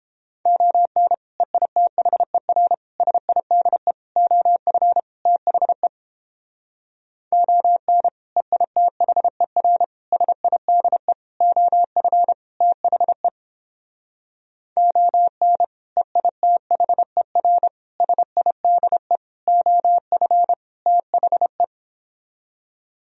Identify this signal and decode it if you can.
Morse code, heard as a keyed tone, about 31 wpm, 700 Hz. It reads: ON EITHER SIDE OF THE ON EITHER SIDE OF THE ON EITHER SIDE OF THE